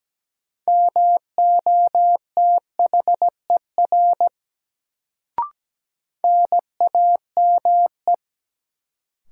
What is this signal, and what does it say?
Morse code, heard as a keyed tone, about 17 wpm, 700 Hz. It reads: MOTHER NAME
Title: Sentences from Top 200 Words - New Word Drill - 17wpm